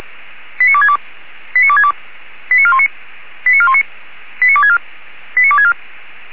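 Five-tone selective call (ZVEI), heard as quick runs of five tones, tones of about 70 ms. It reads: 89262 89262 89419 89419 89264 89264
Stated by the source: ZVEI five-tone selective call received on an SDR.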